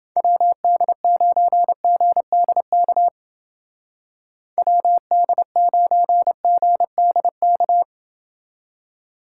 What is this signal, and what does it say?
Morse code, heard as a keyed tone, about 29 wpm, 700 Hz. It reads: WD9GDK WD9GDK